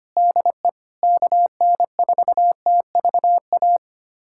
Morse code, heard as a keyed tone, about 25 wpm, 700 Hz. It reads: DE KN4TVA
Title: USA Callsign DE Callsign 25wpm